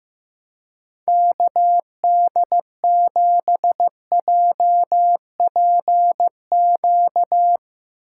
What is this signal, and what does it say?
Morse code, heard as a keyed tone, about 15 wpm, 700 Hz. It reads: KD7JPQ
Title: Speed Racing - USA 2x3 Call Signs 15wpm